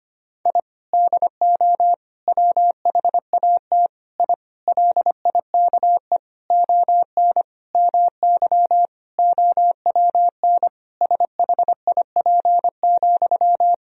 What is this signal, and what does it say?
Morse code, heard as a keyed tone, about 25 wpm, 700 Hz. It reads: I DO WHAT I LIKE ON MY OWN SHIP,
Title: Sentences from Top 1000 Words - Review Part 2 - 25wpm